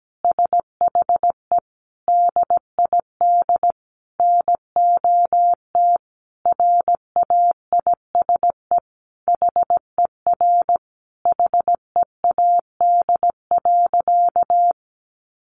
Morse code, heard as a keyed tone, about 17 wpm, 700 Hz. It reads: SHE DID NOT RAISE HER HEAD.
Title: Sentences from Top 1400 Words 17wpm